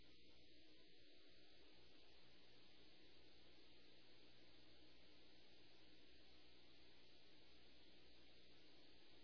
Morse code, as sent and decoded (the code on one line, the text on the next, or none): none